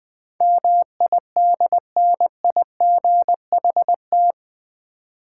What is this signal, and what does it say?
Morse code, heard as a keyed tone, about 20 wpm, 700 Hz. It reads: MIDNIGHT
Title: 8 Letter Words - First 3 Letters Given - 20wpm